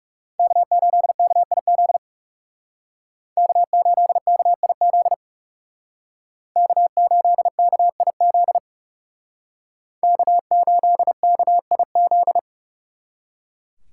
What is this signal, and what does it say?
Morse code, heard as a keyed tone, about 37 wpm, 700 Hz. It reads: K8KIZ K8KIZ K8KIZ K8KIZ
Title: Speed Racing - USA Call Signs 30wpm